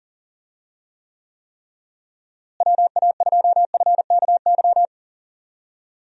Morse code, heard as a keyed tone, about 37 wpm, 700 Hz. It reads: WA2FKY
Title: USA Call Signs - 6 Characters - Repeated 3x - 40wpm